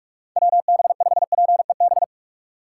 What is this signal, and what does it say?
Morse code, heard as a keyed tone, about 43 wpm, 700 Hz. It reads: WB5PEB
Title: USA Call Signs - 6 Characters - 4x Character Spacing - 45wpm